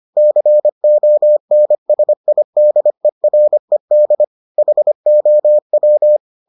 Morse code, heard as a keyed tone, about 25 wpm, 600 Hz. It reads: CONSIDERED HOW